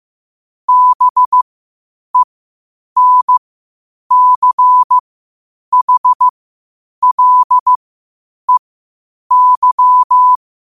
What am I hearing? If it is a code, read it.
Morse code, heard as a keyed tone, about 15 wpm, spaced as for 8 wpm, 1000 Hz. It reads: BENCHLEY
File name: qotdfiles / 8wpm